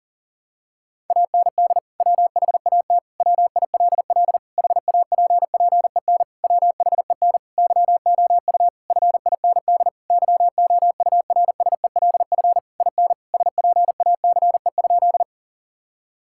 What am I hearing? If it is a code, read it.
Morse code, heard as a keyed tone, about 37 wpm, 700 Hz. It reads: AND WHAT WILL HAPPEN WHEN YOU FIND YOURSELF IN SPACE?
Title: Sentences from Top 1000 Words - Review Part 5 - 40wpm